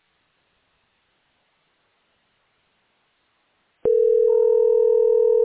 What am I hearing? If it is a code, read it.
Ringback tone (North America), 440 + 480 Hz.